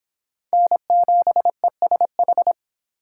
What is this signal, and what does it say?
Morse code, heard as a keyed tone, about 26 wpm, 700 Hz. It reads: N7ESH